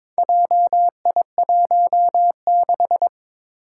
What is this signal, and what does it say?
Morse code, heard as a keyed tone, about 22 wpm, 700 Hz. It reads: JI16
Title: Four Challenging Letter-Number Combinations 22wpm